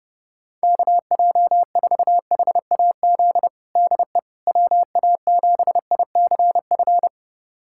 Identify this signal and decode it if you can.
Morse code, heard as a keyed tone, about 30 wpm, 700 Hz. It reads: KJ4HAZ DE WA7ICF
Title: USA Callsign DE Callsign - 6 Characters - 30wpm